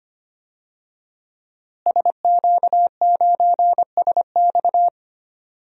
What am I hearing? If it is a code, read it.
Morse code, heard as a keyed tone, about 25 wpm, 700 Hz. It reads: SQ9SX
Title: DXCC-100 Call Signs - 5 Characters - Repeated 3x - 25wpm